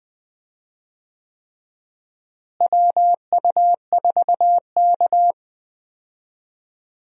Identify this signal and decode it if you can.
Morse code, heard as a keyed tone, about 21 wpm, 700 Hz. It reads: WU4K
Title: USA Call Signs - 4 Characters - Repeated 3x - 20wpm